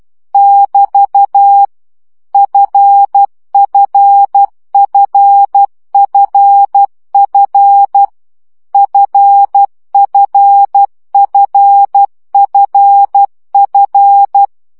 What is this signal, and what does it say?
Morse code, heard as a keyed tone, about 12 wpm, 800 Hz. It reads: <BT> FFFFF FFFFF
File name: CW gyakorlas f 10 baud